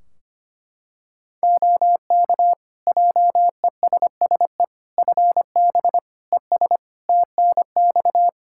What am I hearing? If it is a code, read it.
Morse code, heard as a keyed tone, about 25 wpm, 700 Hz. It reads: OK JESSE FB ES TNX
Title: Practice QSO - Exchange 2 - Info - 25wpm